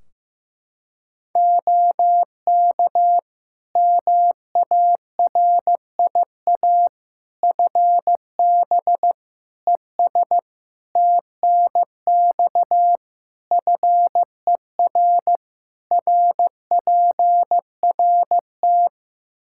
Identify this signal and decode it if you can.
Morse code, heard as a keyed tone, about 15 wpm, 700 Hz. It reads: OK MARIA FB ES TNX FER RPRT
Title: Practice QSO - Exchange 2 - Info - 15wpm